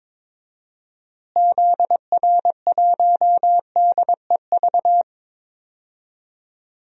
Morse code, heard as a keyed tone, about 22 wpm, 700 Hz. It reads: ZR1DEV
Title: DXCC-100 Call Signs - 6 Characters - Repeated 3x - 22wpm